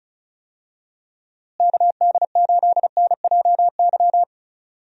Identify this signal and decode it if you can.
Morse code, heard as a keyed tone, about 34 wpm, 700 Hz. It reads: KD8NJY
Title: USA Call Signs - 6 Characters - Repeated 2x - 35wpm